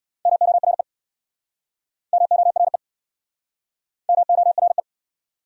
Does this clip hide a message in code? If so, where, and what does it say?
Yes: Morse code from 0.3 to 4.8 s, heard as a keyed tone, about 66 wpm, 700 Hz. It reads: MORE MORE MORE